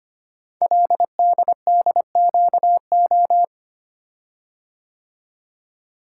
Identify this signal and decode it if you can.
Morse code, heard as a keyed tone, about 25 wpm, 700 Hz. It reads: LDDQO
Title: Code Groups - 5 Letters - 25wpm